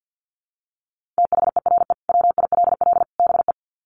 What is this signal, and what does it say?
Morse code, heard as a keyed tone, about 50 wpm, 700 Hz. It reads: THERE WILL BE
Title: Sets of 3 Words 50wpm